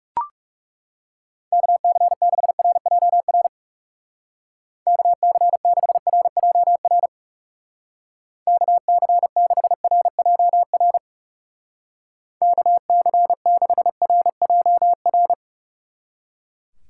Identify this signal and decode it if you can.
Morse code, heard as a keyed tone, about 36 wpm, 700 Hz. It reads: KC6RJR KC6RJR KC6RJR KC6RJR
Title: Speed Racing - USA 2x3 Call Signs 30wpm